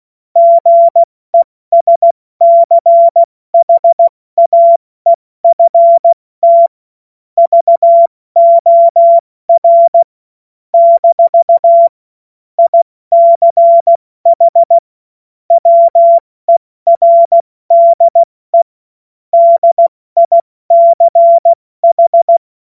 Morse code, heard as a keyed tone, about 16 wpm, 680 Hz. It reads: GESCHAEFT VOR - ICH WERDE DICH
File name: Morsekurs / Witze / Witz 09. August 2022 70 F80